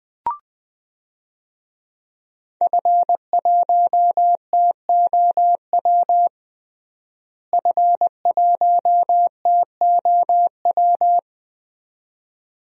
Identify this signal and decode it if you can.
Morse code, heard as a keyed tone, about 20 wpm, 700 Hz. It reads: F1TOW F1TOW
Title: DXCC-100 Call Signs - 5 Characters - Repeated 2x - 20wpm